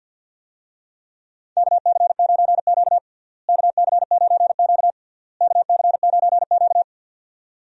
Morse code, heard as a keyed tone, about 53 wpm, 700 Hz. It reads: KC9Q KC9Q KC9Q
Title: USA Call Signs - 4 Characters - Repeated 3x - 50wpm